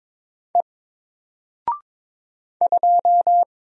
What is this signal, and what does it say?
Morse code, heard as a keyed tone, about 22 wpm, 700 Hz. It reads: E 2
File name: Single-Letter-Number-22wpm